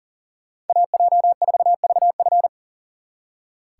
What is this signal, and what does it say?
Morse code, heard as a keyed tone, about 52 wpm, 700 Hz. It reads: AJ4VF